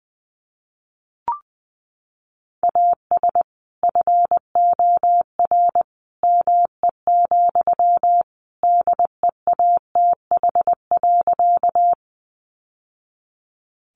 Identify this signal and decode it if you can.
Morse code, heard as a keyed tone, about 20 wpm, 700 Hz. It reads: AS FOR ME, DEATH.
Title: Sentences from Top 1100 Words 20wpm